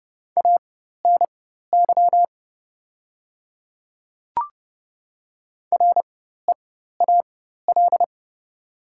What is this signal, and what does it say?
Morse code, heard as a keyed tone, about 31 wpm, spaced as for 14 wpm, 700 Hz. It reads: ANY REAL
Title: 4x Character Spacing - Top 500 Words - 30wpm